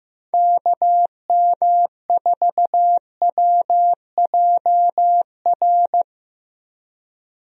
Morse code, heard as a keyed tone, about 15 wpm, 700 Hz. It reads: KM4WJR
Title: USA Call Signs - 6 Characters - Repeated 2x - 15wpm